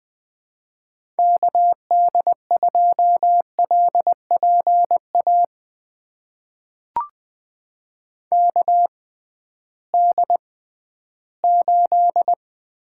Morse code, heard as a keyed tone, about 21 wpm, spaced as for 13 wpm, 700 Hz. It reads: KD2LPA KD8
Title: USA Call Signs - 6 Characters - 6x Character Spacing - 20wpm